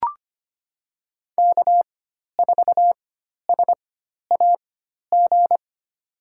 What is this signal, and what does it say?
Morse code, heard as a keyed tone, about 25 wpm, spaced as for 12 wpm, 700 Hz. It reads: K4SAG
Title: USA Call Signs - 4x Character Spacing - 25wpm